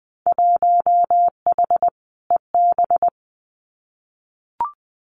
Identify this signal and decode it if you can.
Morse code, heard as a keyed tone, about 20 wpm, 700 Hz. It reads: 1H EB
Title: Winter Field Day Exchanges 20wpm